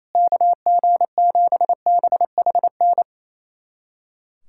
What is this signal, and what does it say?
Morse code, heard as a keyed tone, about 28 wpm, 700 Hz. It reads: KG7BHN
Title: USA Callsign DE Callsign - 6 Characters - 28wpm